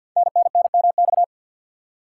Morse code, heard as a keyed tone, about 50 wpm, 700 Hz. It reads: NNNMX
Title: Code Groups - 5 Letters - 50wpm